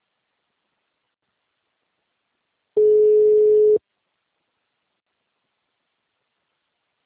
Ringback tone (Europe, 425 Hz).
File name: out- -2003- - - .42287